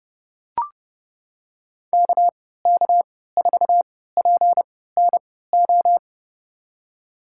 Morse code, heard as a keyed tone, about 30 wpm, spaced as for 17 wpm, 700 Hz. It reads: KK4PNO
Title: USA Call Signs - 6 Characters - 3x Character Spacing - 30wpm